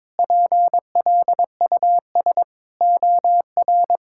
Morse code, heard as a keyed tone, about 22 wpm, 700 Hz. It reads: PLUS OR
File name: practice / 22wpm / Sets of 8 Words - 22wpm